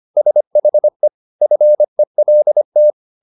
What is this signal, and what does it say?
Morse code, heard as a keyed tone, about 25 wpm, 600 Hz. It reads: SHE FELT